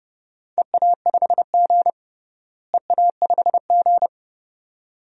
Morse code, heard as a keyed tone, about 30 wpm, 700 Hz. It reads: EA5G EA5G
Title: DXCC-100 Call Signs - 4 Characters - Repeated 2x - 30wpm